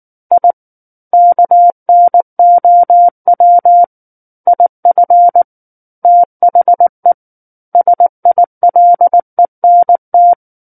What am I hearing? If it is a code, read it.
Morse code, heard as a keyed tone, about 19 wpm, 700 Hz. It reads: I KNOW IF THE SILENT
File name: Book One / 10-Chapter 10 - 19 WPM